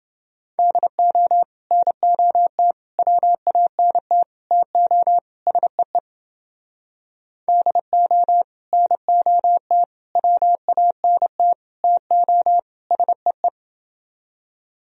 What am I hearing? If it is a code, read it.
Morse code, heard as a keyed tone, about 30 wpm, 700 Hz. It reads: DO NOT WANT TO SEE DO NOT WANT TO SEE